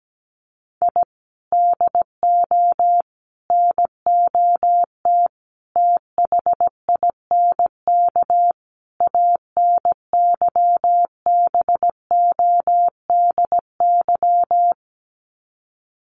Morse code, heard as a keyed tone, about 17 wpm, 700 Hz. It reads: I DO NOT THINK ANYBODY
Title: Sets of 5 Words 17wpm